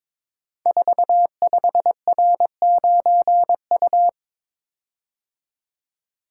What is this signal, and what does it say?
Morse code, heard as a keyed tone, about 22 wpm, 700 Hz. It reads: 45R9U
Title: Code Groups - 5 Letter:Numbers - 22wpm